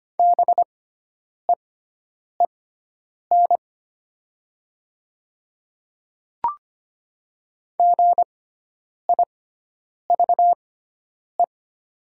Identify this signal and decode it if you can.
Morse code, heard as a keyed tone, about 25 wpm, spaced as for 9 wpm, 700 Hz. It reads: BEEN GIVE
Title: 6x Character Spacing - Top 500 Words - 25wpm